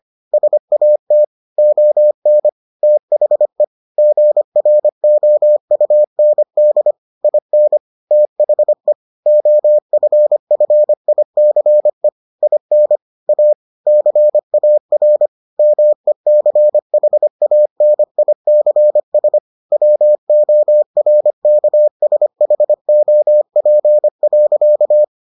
Morse code, heard as a keyed tone, about 25 wpm, 600 Hz. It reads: SAT ON THE GROUND IN THE OFFICE IN A CAR MECHANICS WORKSHOP.